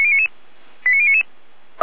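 Five-tone selective call (ZVEI), heard as quick runs of five tones, tones of about 70 ms. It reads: 89099
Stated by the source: ZVEI five-tone selective call received by SDR.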